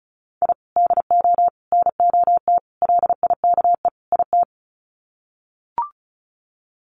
Morse code, heard as a keyed tone, about 34 wpm, 700 Hz. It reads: I DO NOT LIKE IT
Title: Sets of 5 Words Encore 35wpm